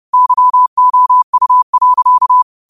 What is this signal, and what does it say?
Morse code, heard as a keyed tone, about 30 wpm, 1000 Hz. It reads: YOU.
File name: qotdfiles / 30wpm